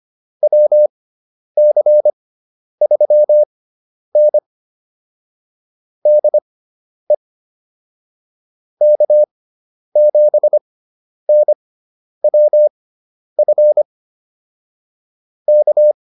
Morse code, heard as a keyed tone, about 25 wpm, spaced as for 10 wpm, 600 Hz. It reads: WC3N DE K7NWF K